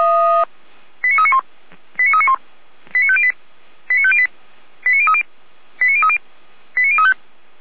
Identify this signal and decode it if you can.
Five-tone selective call (ZVEI), heard as quick runs of five tones, tones of about 70 ms. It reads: 89391 89391 89508 89508 89030 89030 89035
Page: /455/SDR/ZVEI/89042